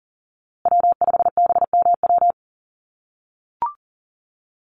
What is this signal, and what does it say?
Morse code, heard as a keyed tone, about 40 wpm, 700 Hz. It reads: W5BMW
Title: USA Call Signs Repeated 2x 40wpm